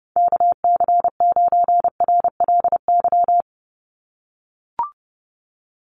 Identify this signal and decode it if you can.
Morse code, heard as a keyed tone, about 30 wpm, 700 Hz. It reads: KC9RLY
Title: USA Call Signs 30wpm